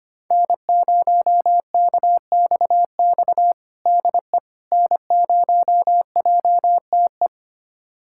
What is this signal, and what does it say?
Morse code, heard as a keyed tone, about 25 wpm, 700 Hz. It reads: N0KXX DE N0JTE
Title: USA Callsign DE Callsign - 5 Characters - 25wpm